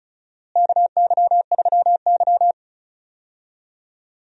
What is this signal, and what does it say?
Morse code, heard as a keyed tone, about 36 wpm, 700 Hz. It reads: KY3Y
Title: USA Call Signs - 4 Characters - Repeated 2x - 35wpm